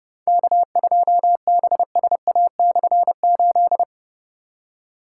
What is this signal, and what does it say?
Morse code, heard as a keyed tone, about 30 wpm, 700 Hz. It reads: K2BSA/8
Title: K1USN SST QSO Practice - 30wpm